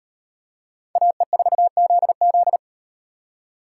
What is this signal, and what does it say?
Morse code, heard as a keyed tone, about 35 wpm, 700 Hz. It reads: AE4ZZ